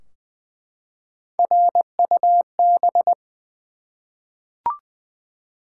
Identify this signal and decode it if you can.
Morse code, heard as a keyed tone, about 20 wpm, 700 Hz. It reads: RUB
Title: Three Letter Words - 20wpm